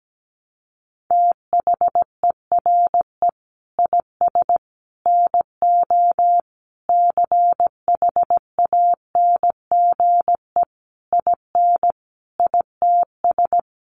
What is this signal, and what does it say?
Morse code, heard as a keyed tone, about 17 wpm, 700 Hz. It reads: THERE IS NO CHANGE IN ITS
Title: Sentences from Top 1300 Words 17wpm